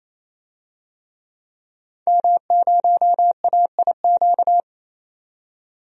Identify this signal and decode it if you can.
Morse code, heard as a keyed tone, about 28 wpm, 700 Hz. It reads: M0AIQ
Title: DXCC-100 Call Signs - 5 Characters - 28wpm